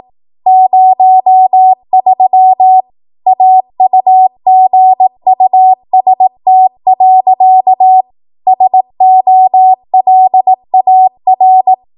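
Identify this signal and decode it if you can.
Morse code, heard as a keyed tone, about 18 wpm, 750 Hz. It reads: 03 AUGUST. SOLAR
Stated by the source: ARRL 18 wpm Morse code practice archive transmission.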